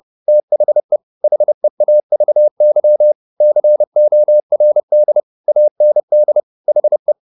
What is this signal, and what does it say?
Morse code, heard as a keyed tone, about 30 wpm, 600 Hz. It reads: THE HEAVY CORD AND HE
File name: the old man and the sea / the old man and the sea 1 30wpm